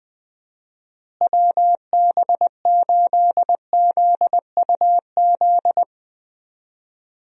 Morse code, heard as a keyed tone, about 20 wpm, 700 Hz. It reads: WB8ZUZ
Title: USA Call Signs - 6 Characters - Repeated 2x - 20wpm